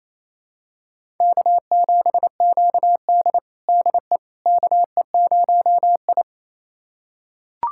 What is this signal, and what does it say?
Morse code, heard as a keyed tone, about 27 wpm, 700 Hz. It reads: K7QD DE KE0I
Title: USA Callsign DE Callsign - 4 Characters - 28wpm